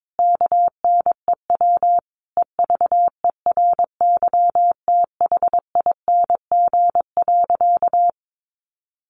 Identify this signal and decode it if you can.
Morse code, heard as a keyed tone, about 22 wpm, 700 Hz. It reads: KNEW EVERYTHING.
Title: Sentences from Top 700 Words 22wpm